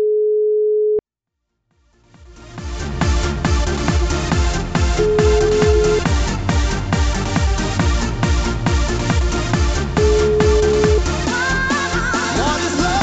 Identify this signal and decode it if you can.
Ringback tone (Europe, 425 Hz), with noise.